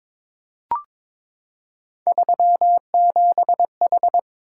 Morse code, heard as a keyed tone, about 22 wpm, 700 Hz. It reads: 37H
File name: practice / 22wpm / Three Letter-Number Combinations - 22wpm